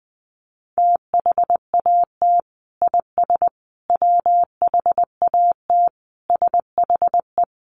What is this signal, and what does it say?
Morse code, heard as a keyed tone, about 20 wpm, 700 Hz. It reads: THAT IS WHAT SHE
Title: Sentences from Top 100 Words 20wpm